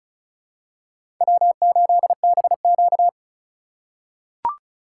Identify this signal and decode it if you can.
Morse code, heard as a keyed tone, about 36 wpm, 700 Hz. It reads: W8BQ
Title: USA Call Signs - 4 Characters - 4x Character Spacing - 35wpm